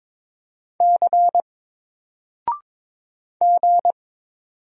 Morse code, heard as a keyed tone, about 22 wpm, 700 Hz. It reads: C G
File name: Single-Letter-Number-22wpm